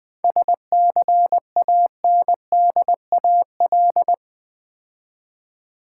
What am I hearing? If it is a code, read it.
Morse code, heard as a keyed tone, about 20 wpm, 700 Hz. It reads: SCANDAL
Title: 7 Letter Words - First 3 Letters Given - 20wpm